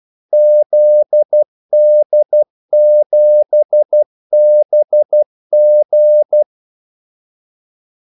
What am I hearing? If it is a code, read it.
Morse code, heard as a keyed tone, about 12 wpm, 600 Hz. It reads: ZD7BG
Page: Kaldesignaler 12 wpm